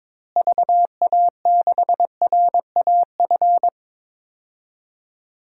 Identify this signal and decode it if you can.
Morse code, heard as a keyed tone, about 22 wpm, 700 Hz. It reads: VA6RAF
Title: DXCC-100 Call Signs - 6 Characters - Repeated 2x - 22wpm